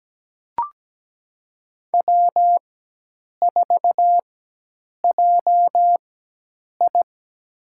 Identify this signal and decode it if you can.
Morse code, heard as a keyed tone, about 17 wpm, spaced as for 8 wpm, 700 Hz. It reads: W4JI
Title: USA Call Signs - 4 Characters - 4x Character Spacing - 17wpm